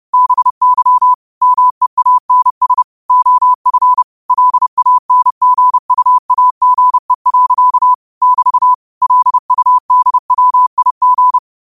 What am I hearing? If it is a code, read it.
Morse code, heard as a keyed tone, about 30 wpm, 1000 Hz. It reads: BY MEANS OF LANGUAGE. <BT> LUDWIG
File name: qotdfiles / 30wpm